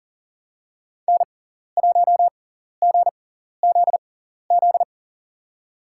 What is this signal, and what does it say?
Morse code, heard as a keyed tone, about 40 wpm, spaced as for 14 wpm, 700 Hz. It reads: N1GZZ
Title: USA Call Signs - 5 Characters - 6x Character Spacing - 40wpm